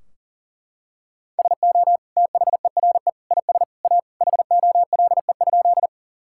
Morse code, heard as a keyed tone, about 40 wpm, 700 Hz. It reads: SO THERE IS A HOLE?
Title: Sentences from Top 1600 Words 40wpm